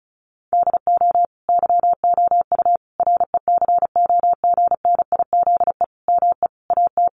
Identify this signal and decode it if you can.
Morse code, heard as a keyed tone, about 34 wpm, 700 Hz. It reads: DO YOU RECOGNIZE ME AT
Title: Sentences from Top 1400 Words 35wpm